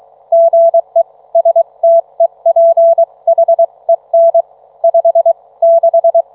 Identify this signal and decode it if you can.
Morse code, heard as a keyed tone, about 23 wpm, 660 Hz. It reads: GE STEPHEN 5 6